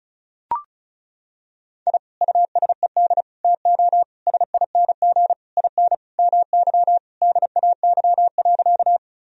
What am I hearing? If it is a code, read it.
Morse code, heard as a keyed tone, about 36 wpm, 700 Hz. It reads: I USED TO SING IN MY DAY.
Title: Sentences from Top 1000 Words - Review Part 2 - 35wpm